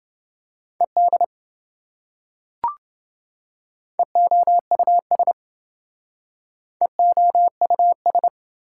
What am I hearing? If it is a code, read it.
Morse code, heard as a keyed tone, about 27 wpm, 700 Hz. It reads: ED EOUS EOUS